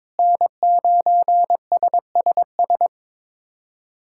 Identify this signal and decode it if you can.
Morse code, heard as a keyed tone, about 22 wpm, 700 Hz. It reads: N9SSS
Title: USA Callsign DE Callsign 22wpm